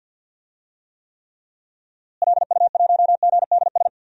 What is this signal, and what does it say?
Morse code, heard as a keyed tone, about 52 wpm, 700 Hz. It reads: LU1GDS
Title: DXCC-100 Call Signs - 6 Characters - Repeated 3x - 50wpm